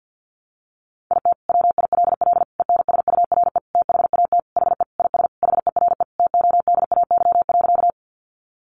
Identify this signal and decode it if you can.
Morse code, heard as a keyed tone, about 50 wpm, 700 Hz. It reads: IT WILL ENSURE THAT HE IS HERE TODAY.
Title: Sentences from Top 1100 Words 50wpm